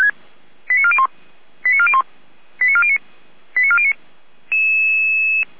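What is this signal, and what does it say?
Five-tone selective call (ZVEI), heard as quick runs of five tones, tones of about 70 ms. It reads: 89401 89401 89409 89409
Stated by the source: ZVEI five-tone call 89080 received by SDR.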